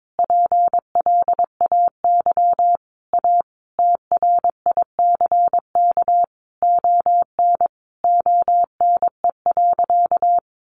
Morse code, heard as a keyed tone, about 22 wpm, 700 Hz. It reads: PLAY A TRICK ON ONE.
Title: Sentences from Top 1500 Words 22wpm